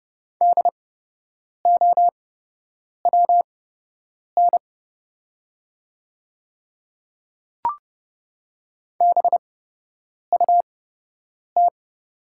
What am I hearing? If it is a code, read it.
Morse code, heard as a keyed tone, about 30 wpm, spaced as for 8 wpm, 700 Hz. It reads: DOWN BUT
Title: 8x Character Spacing - Top 100 Words - 30wpm